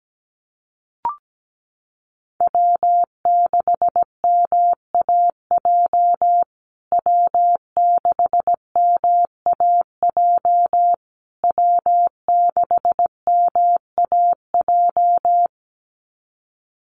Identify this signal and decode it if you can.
Morse code, heard as a keyed tone, about 17 wpm, 700 Hz. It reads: W6MAJ W6MAJ W6MAJ
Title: USA Call Signs Repeated 3x 17wpm